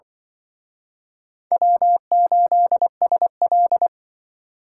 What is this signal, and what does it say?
Morse code, heard as a keyed tone, about 24 wpm, 700 Hz. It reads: W8SL
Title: Speed Racing - USA 1x2 Call Signs 20wpm